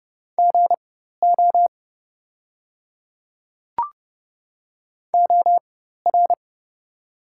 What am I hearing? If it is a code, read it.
Morse code, heard as a keyed tone, about 30 wpm, spaced as for 14 wpm, 700 Hz. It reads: GO OR
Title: 4x Character Spacing - Top 200 Words - 30wpm